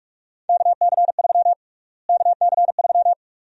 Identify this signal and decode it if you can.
Morse code, heard as a keyed tone, about 47 wpm, 700 Hz. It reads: KC3 KC3
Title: USA Call Sign Prefixes - Repeated 2x - 45wpm